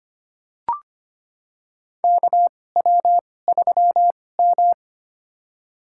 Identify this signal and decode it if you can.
Morse code, heard as a keyed tone, about 25 wpm, spaced as for 18 wpm, 700 Hz. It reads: KW3M
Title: USA Call Signs - 4 Characters - 2x Character Spacing - 25wpm